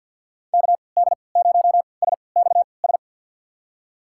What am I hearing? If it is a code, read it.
Morse code, heard as a keyed tone, about 53 wpm, spaced as for 29 wpm, 700 Hz. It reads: KD0SXS
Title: USA Call Signs - 6 Characters - 3x Character Spacing - 50wpm